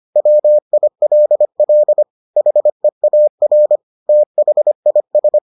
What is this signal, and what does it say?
Morse code, heard as a keyed tone, about 25 wpm, 600 Hz. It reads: WILL HEAR THIS